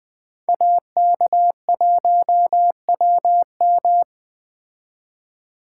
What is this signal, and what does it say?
Morse code, heard as a keyed tone, about 20 wpm, 700 Hz. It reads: AK1WM
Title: Speed Racing - USA 2x2 Call Signs 17wpm